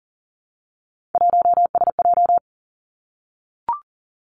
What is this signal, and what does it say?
Morse code, heard as a keyed tone, about 42 wpm, 700 Hz. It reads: 1SJ
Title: Three Challenging Letter-Number Combinations 40wpm